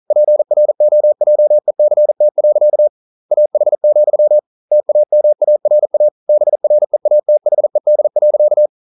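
Morse code, heard as a keyed tone, about 42 wpm, 600 Hz. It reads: PROJECT. AH, TAMARA BREATHED.
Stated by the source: Morse code at 40 wpm, practice text, CW med Gnister.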